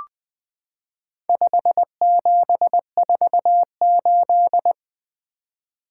Morse code, heard as a keyed tone, about 21 wpm, 700 Hz. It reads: 5748
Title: Four Letter-Number Combinations - 20wpm